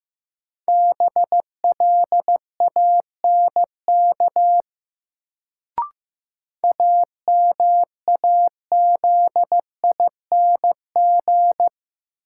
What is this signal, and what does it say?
Morse code, heard as a keyed tone, about 15 wpm, 700 Hz. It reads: BLANK AMAZING
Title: Sentences from Top 1500 Words - New Word Drill Alt - 15wpm